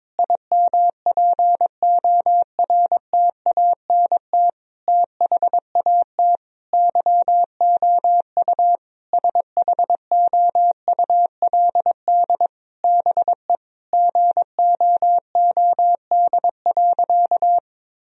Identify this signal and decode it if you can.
Morse code, heard as a keyed tone, about 22 wpm, 700 Hz. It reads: IMPORTANT THAT YOU SHOULD BE GOOD.
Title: Sentences from Top 1000 Words - Review Part 1 - 22wpm